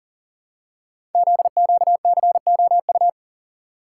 Morse code, heard as a keyed tone, about 39 wpm, 700 Hz. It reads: ZQCOU
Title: Code Groups - 5 Letters - 40wpm